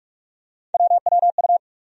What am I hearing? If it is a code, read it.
Morse code, heard as a keyed tone, about 43 wpm, 700 Hz. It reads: WWU
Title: USA Call Sign Postfixes - 3 Letters 45wpm